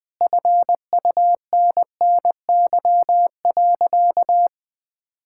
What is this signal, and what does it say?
Morse code, heard as a keyed tone, about 20 wpm, 700 Hz. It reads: FUNNY.
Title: Sentences with 2-Word Phrases - Spoken Beginning - 20wpm